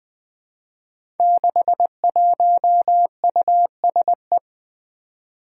Morse code, heard as a keyed tone, about 20 wpm, 700 Hz. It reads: 61USE
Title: Five Challenging Letter-Number Combinations 20wpm